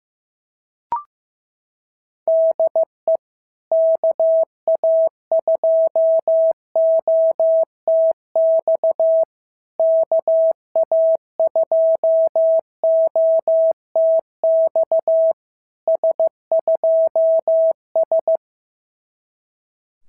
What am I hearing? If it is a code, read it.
Morse code, heard as a keyed tone, about 15 wpm, 650 Hz. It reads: DE KA2OTX KA2OTX S2S
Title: SOTA QSO Practice 15wpm